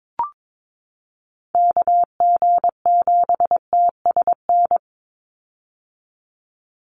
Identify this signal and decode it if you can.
Morse code, heard as a keyed tone, about 22 wpm, 700 Hz. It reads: KG7TSN
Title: USA Call Signs 22wpm